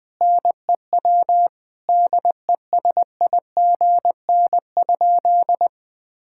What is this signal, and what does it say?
Morse code, heard as a keyed tone, about 20 wpm, 700 Hz. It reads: NEW DESIGN?
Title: Sentences with 5-Word Phrases - 20wpm